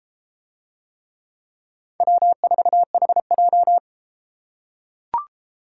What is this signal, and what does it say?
Morse code, heard as a keyed tone, about 32 wpm, 700 Hz. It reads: W4HJ